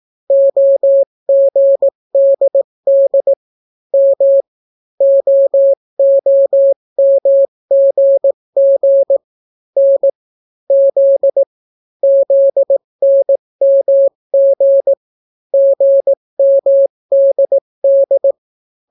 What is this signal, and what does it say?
Morse code, heard as a keyed tone, about 18 wpm, 550 Hz. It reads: OGDD M OOMGG N Z ZNMG GMDD